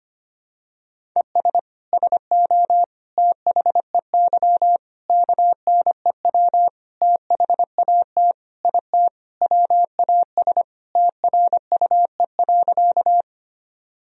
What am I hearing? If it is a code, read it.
Morse code, heard as a keyed tone, about 25 wpm, 700 Hz. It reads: ES SO THEY KNEW THAT IT WAS TRUE.
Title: Sentences from Top 400 Words and Abbreviations 25wpm